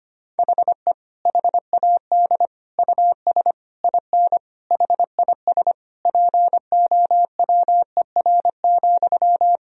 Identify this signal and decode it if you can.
Morse code, heard as a keyed tone, about 25 wpm, 700 Hz. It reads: HE HAD US IN HIS POWER,
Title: Sentences from Top 1000 Words - Review Part 2 - 25wpm